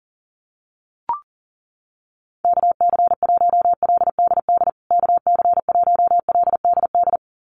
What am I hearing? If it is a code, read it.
Morse code, heard as a keyed tone, about 40 wpm, 700 Hz. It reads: KC1LDD KC1LDD
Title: USA Call Signs Repeated 2x 40wpm